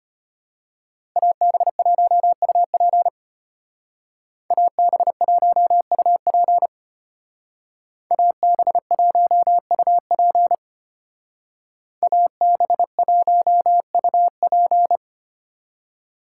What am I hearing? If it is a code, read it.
Morse code, heard as a keyed tone, about 33 wpm, 700 Hz. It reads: AB1UP AB1UP AB1UP AB1UP